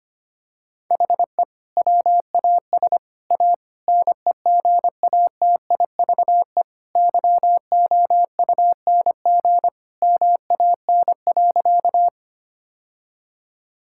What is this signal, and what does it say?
Morse code, heard as a keyed tone, about 25 wpm, 700 Hz. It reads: HE WAS A NEGATIVE YOUNG MAN.
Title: Sentences from Top 1000 Words - Review Part 6 - 25wpm